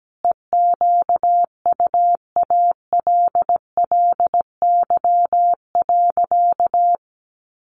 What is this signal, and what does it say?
Morse code, heard as a keyed tone, about 17 wpm, 700 Hz. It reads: EQUALLY.
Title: Sentences from Top 1400 Words 17wpm